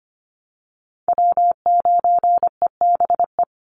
Morse code, heard as a keyed tone, about 25 wpm, 700 Hz. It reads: W9EBE
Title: Call Signs Repeated 3x - 25wpm